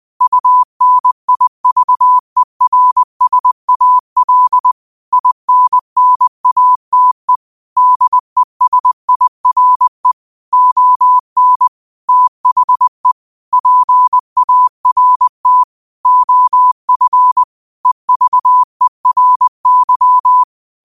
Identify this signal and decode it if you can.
Morse code, heard as a keyed tone, about 20 wpm, 1000 Hz. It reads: UNIVERSAL INNATE DESIRE ON THE PART OF EVERY
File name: qotdfiles / 20wpm